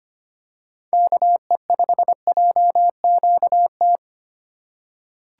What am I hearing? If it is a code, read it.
Morse code, heard as a keyed tone, about 25 wpm, 700 Hz. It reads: KE5JQT
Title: USA Call Signs - 4x Character Spacing - 25wpm